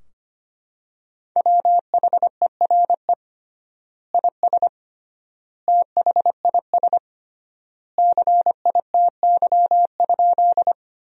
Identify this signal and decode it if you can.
Morse code, heard as a keyed tone, about 25 wpm, 700 Hz. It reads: WHERE IS THIS CITY?
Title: 3x Word-Spacing - Sentences from Top 300 Words - 25wpm